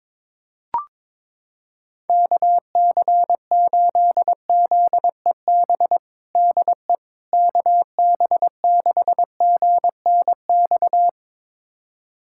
Morse code, heard as a keyed tone, about 22 wpm, 700 Hz. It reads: KC8ZEB DE KB6GNX
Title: USA Callsign DE Callsign 22wpm